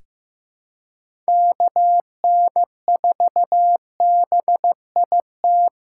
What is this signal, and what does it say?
Morse code, heard as a keyed tone, about 15 wpm, 700 Hz. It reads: KN4BIT
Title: USA Callsign DE Callsign - 6 Characters - 15wpm